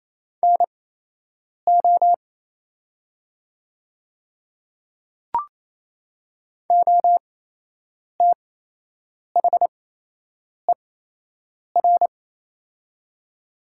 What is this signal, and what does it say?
Morse code, heard as a keyed tone, about 28 wpm, spaced as for 8 wpm, 700 Hz. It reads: NO OTHER
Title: 8x Character Spacing - Top 200 Words - 28wpm